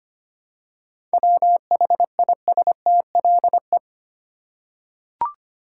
Morse code, heard as a keyed tone, about 25 wpm, 690 Hz. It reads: WHISTLE
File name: practice / 25wpm / Easy Words with Hints - 25wpm